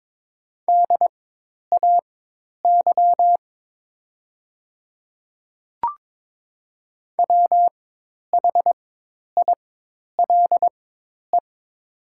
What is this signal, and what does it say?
Morse code, heard as a keyed tone, about 22 wpm, spaced as for 10 wpm, 700 Hz. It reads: DAY WHILE
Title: 4x Character Spacing - Top 300 Words - 22wpm